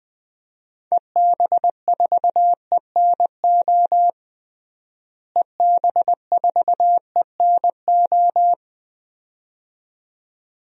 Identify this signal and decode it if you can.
Morse code, heard as a keyed tone, about 20 wpm, 700 Hz. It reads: EB4ENO EB4ENO
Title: DXCC-100 Call Signs - 6 Characters - Repeated 2x - 20wpm